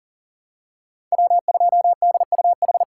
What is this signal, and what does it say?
Morse code, heard as a keyed tone, about 40 wpm, 700 Hz. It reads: W2DUH
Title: USA Call Signs - 3x Character Spacing - 40wpm